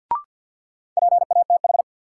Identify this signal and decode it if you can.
Morse code, heard as a keyed tone, about 49 wpm, 700 Hz. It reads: PATH
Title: Sentences from Top 1400 Words - New Word Drill Alt - 50wpm